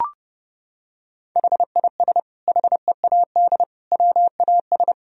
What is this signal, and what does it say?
Morse code, heard as a keyed tone, about 29 wpm, 700 Hz. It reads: HIS HEAD WAS
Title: Sentences from Top 1000 Words - Review Part 1 - 30wpm